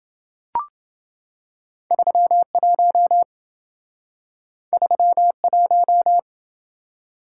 Morse code, heard as a keyed tone, about 26 wpm, 700 Hz. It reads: 31 31